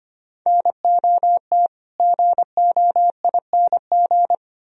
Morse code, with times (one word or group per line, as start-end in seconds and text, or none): 0.5-1.7 NOT
2.0-4.3 GOING